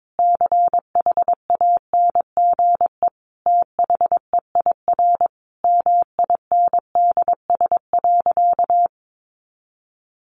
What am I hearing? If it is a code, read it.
Morse code, heard as a keyed tone, about 22 wpm, 700 Hz. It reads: CHANGE THEIR MINDS.
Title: Easy Phrases 22wpm